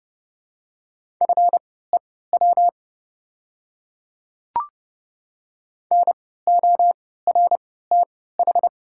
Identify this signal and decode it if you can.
Morse code, heard as a keyed tone, about 30 wpm, spaced as for 17 wpm, 700 Hz. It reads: FEW NORTH
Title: 3x Character Spacing - Top 300 Words - 30wpm